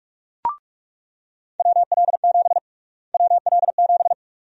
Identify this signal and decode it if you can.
Morse code, heard as a keyed tone, about 46 wpm, 700 Hz. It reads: WL7 WL7
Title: USA Call Sign Prefixes - Repeated 2x - 45wpm